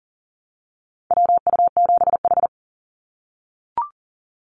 Morse code, heard as a keyed tone, about 40 wpm, 700 Hz. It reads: WU7H
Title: NA SOTA Activator Callsigns 40wpm